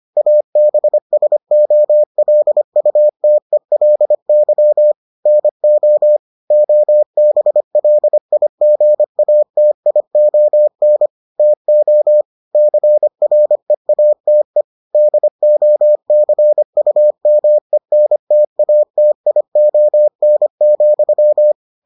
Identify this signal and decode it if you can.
Morse code, heard as a keyed tone, about 25 wpm, 600 Hz. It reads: ABSOLUTELY NO OBLIGATION TO CREATE DOCUMENTATION,